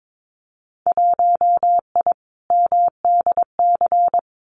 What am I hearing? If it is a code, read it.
Morse code, heard as a keyed tone, about 22 wpm, 700 Hz. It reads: 1I MDC
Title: Winter Field Day Exchanges 22wpm